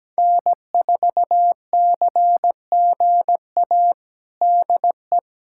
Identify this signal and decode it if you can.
Morse code, heard as a keyed tone, about 17 wpm, 700 Hz. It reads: N4CGA DE
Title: USA Callsign DE Callsign - 5 Characters - 17wpm